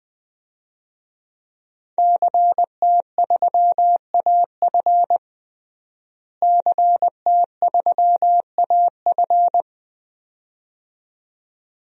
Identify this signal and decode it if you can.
Morse code, heard as a keyed tone, about 20 wpm, 700 Hz. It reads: CT3AF CT3AF
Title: DXCC-100 Call Signs - 5 Characters - Repeated 2x - 20wpm